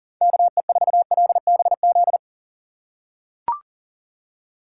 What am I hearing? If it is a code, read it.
Morse code, heard as a keyed tone, about 40 wpm, 700 Hz. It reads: KE4LBZ
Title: USA Call Signs - 6 Characters - 2x Character Spacing - 40wpm